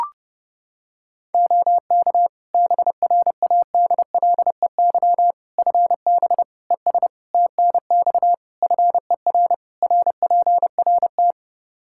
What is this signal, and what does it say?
Morse code, heard as a keyed tone, about 32 wpm, 700 Hz. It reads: OK BRADLEY FB ES TNX FER RPRT
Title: Practice QSO - Exchange 2 - Info - 30wpm